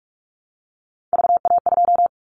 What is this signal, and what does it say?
Morse code, heard as a keyed tone, about 46 wpm, 700 Hz. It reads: VA2
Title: Callsign Prefixes for Top 50 DXCC - 45wpm